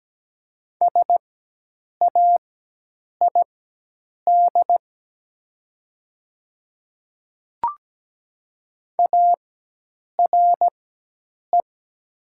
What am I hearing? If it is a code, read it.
Morse code, heard as a keyed tone, about 17 wpm, spaced as for 8 wpm, 700 Hz. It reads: SAID ARE
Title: 4x Character Spacing - Top 100 Words - 17wpm